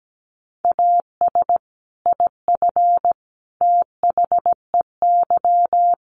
Morse code, heard as a keyed tone, about 17 wpm, 700 Hz. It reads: AS IF THEY
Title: Sets of 4 Words 17wpm